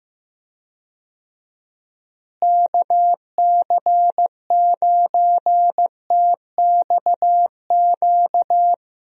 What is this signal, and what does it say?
Morse code, heard as a keyed tone, about 15 wpm, 700 Hz. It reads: KC9TXQ
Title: USA Call Signs - 6 Characters - Repeated 2x - 15wpm